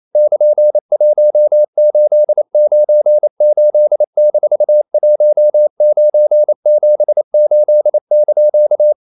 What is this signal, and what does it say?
Morse code, heard as a keyed tone, about 28 wpm, 600 Hz. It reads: <KN>1898-1978)
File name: CW / Practice / EnglishQuotations0128